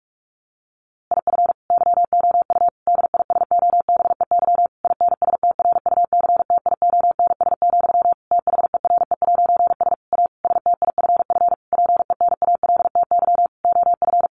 Morse code, heard as a keyed tone, about 44 wpm, 700 Hz. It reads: IF YOU DISOBEY INSTRUCTIONS, THERE'S A STIFF PENALTY OF
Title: A Sound Of Thunder 45wpm